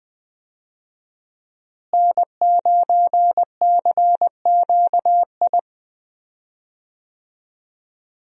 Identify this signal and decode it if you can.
Morse code, heard as a keyed tone, about 20 wpm, 700 Hz. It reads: N9CQI
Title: USA Call Signs - 5 Characters - Repeated 2x - 20wpm